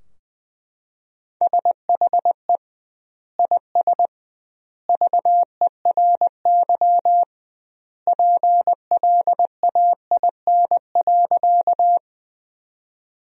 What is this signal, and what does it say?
Morse code, heard as a keyed tone, about 20 wpm, 700 Hz. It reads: SHE IS VERY PLAIN.
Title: 2x Word-Spacing - Sentences from Top 400 Words - 20wpm